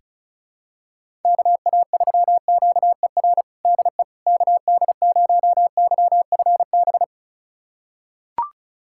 Morse code, heard as a keyed tone, about 34 wpm, 700 Hz. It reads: KA3QER DE KD0YFB
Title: USA Callsign DE Callsign - 6 Characters - 35wpm